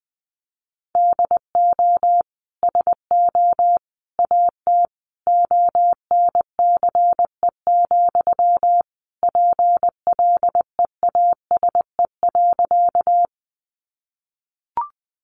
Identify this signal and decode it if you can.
Morse code, heard as a keyed tone, about 19 wpm, 700 Hz. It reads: DO SO AT ONCE, PLEASE.
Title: Sentences from Top 1500 Words 20wpm